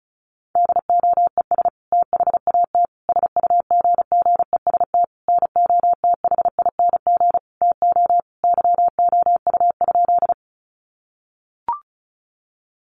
Morse code, heard as a keyed tone, about 34 wpm, 700 Hz. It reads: DOES THAT SUGGEST NOTHING TO YOU?
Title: Sentences from Top 1300 Words 35wpm